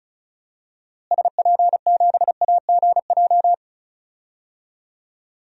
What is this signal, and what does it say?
Morse code, heard as a keyed tone, about 36 wpm, 700 Hz. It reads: SP7AGJ
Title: DXCC-100 Call Signs - 6 Characters - Repeated 2x - 35wpm